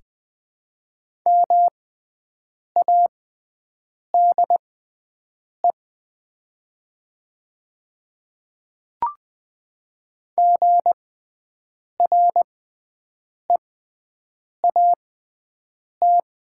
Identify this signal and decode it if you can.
Morse code, heard as a keyed tone, about 20 wpm, spaced as for 7 wpm, 700 Hz. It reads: MADE GREAT
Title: 6x Character Spacing - Top 200 Words - 20wpm